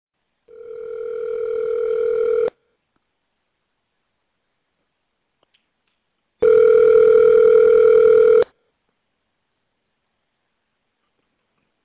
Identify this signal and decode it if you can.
Ringback tone (North America), 440 + 480 Hz.